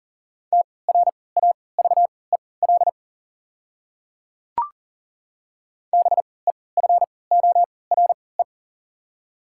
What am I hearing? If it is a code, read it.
Morse code, heard as a keyed tone, about 40 wpm, spaced as for 23 wpm, 690 Hz. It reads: TRAVEL BEFORE
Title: 3x Character Spacing - Top 500 Words - 40wpm